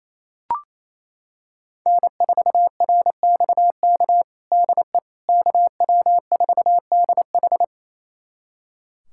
Morse code, heard as a keyed tone, about 27 wpm, 700 Hz. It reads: N4RXK DE KW4DH
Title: USA Callsign DE Callsign - 5 Characters - 28wpm